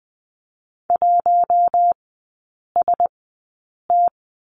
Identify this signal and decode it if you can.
Morse code, heard as a keyed tone, about 20 wpm, spaced as for 8 wpm, 700 Hz. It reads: 1ST
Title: 074 Lesson 13e - Letter L - Three Characters 20wpm